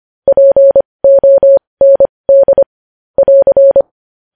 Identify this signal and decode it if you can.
Morse code, heard as a keyed tone, about 25 wpm, spaced as for 20 wpm, 550 Hz. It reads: POND <AR>
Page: News Headlines in Morse Code at 20 WPM